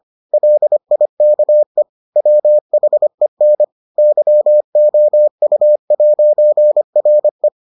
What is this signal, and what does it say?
Morse code, heard as a keyed tone, about 25 wpm, 600 Hz. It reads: LIKE WHEN YOU'RE